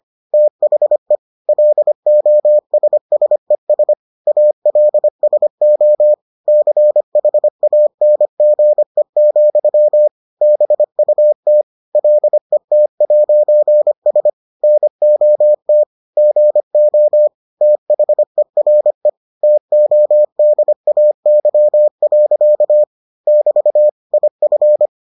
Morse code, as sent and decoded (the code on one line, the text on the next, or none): - .... / .-.. --- ... ... ... / .- .-.. ... --- / -.-. .... .- -. --. --..-- / -... ..- - / .-.. - .----. ... / -. --- - / --. --- / - .... .-. / - --- -.. .- -.-- .-.-.- / -...- / .. ..-.
THE LOSSES ALSO CHANGE, BUT LET'S NOT GO THERE TODAY. <BT> IF